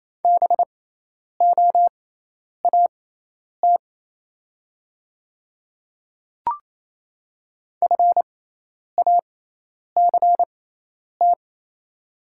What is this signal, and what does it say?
Morse code, heard as a keyed tone, about 28 wpm, spaced as for 10 wpm, 700 Hz. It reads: BOAT FACT
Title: 6x Character Spacing - Top 500 Words - 28wpm